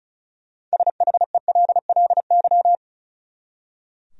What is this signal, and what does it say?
Morse code, heard as a keyed tone, about 34 wpm, 700 Hz. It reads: SHELLY
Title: USA Common Names - 35wpm